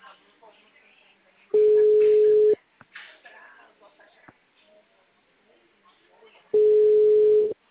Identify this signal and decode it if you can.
Ringback tone (Europe, 425 Hz), with speech.
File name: out- -2016- - - .80211